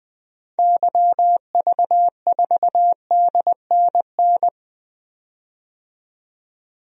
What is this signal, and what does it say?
Morse code, heard as a keyed tone, about 21 wpm, 700 Hz. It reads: YV4DNN
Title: DXCC-100 Call Signs - 6 Characters - 20wpm